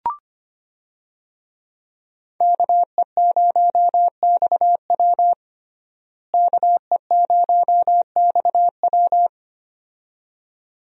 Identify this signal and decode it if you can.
Morse code, heard as a keyed tone, about 25 wpm, 700 Hz. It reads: KE0XW KE0XW
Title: USA Call Signs - 5 Characters - Repeated 2x - 25wpm